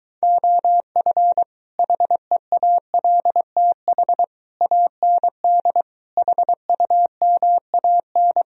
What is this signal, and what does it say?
Morse code, heard as a keyed tone, about 23 wpm, 700 Hz. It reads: OF HEALTH AND HUMAN